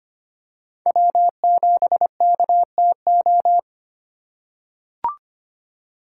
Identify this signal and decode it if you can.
Morse code, heard as a keyed tone, about 25 wpm, 700 Hz. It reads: W7KTO
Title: USA Call Signs - 3x Character Spacing - 25wpm